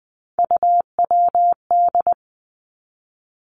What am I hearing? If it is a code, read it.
Morse code, heard as a keyed tone, about 20 wpm, 700 Hz. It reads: UWD
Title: Three Challenging Letter-Number Combinations 20wpm